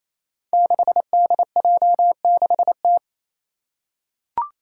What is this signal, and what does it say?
Morse code, heard as a keyed tone, about 28 wpm, 700 Hz. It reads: 6DJ6T
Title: Five Challenging Letter-Number Combinations 28wpm